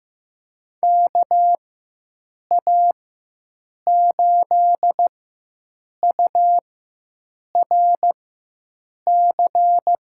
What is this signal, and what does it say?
Morse code, heard as a keyed tone, about 15 wpm, spaced as for 7 wpm, 700 Hz. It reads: KA8URC
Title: USA Call Signs - 6 Characters - 4x Character Spacing - 15wpm